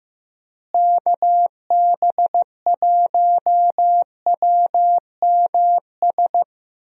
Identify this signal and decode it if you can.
Morse code, heard as a keyed tone, about 15 wpm, 700 Hz. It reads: KB1WMS
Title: USA Call Signs - 6 Characters - Repeated 2x - 15wpm